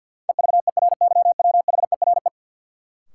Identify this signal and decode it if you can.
Morse code, heard as a keyed tone, about 51 wpm, 700 Hz. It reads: EVERYWHERE
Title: Sentences from Top 1600 Words - New Word Drill - 50wpm